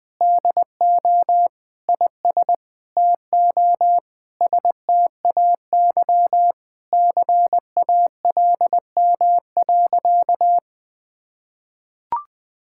Morse code, heard as a keyed tone, about 20 wpm, 700 Hz. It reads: DO IS TO STAY CALM.
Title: Sentences with 7-Word Phrases - Spoken Beginning - 20wpm